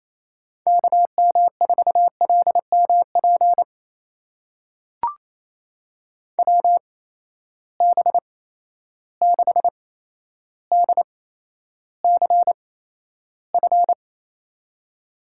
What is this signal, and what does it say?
Morse code, heard as a keyed tone, about 28 wpm, spaced as for 12 wpm, 700 Hz. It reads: KM4LMP WB6DCF
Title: USA Call Signs - 6 Characters - 8x Character Spacing - 28wpm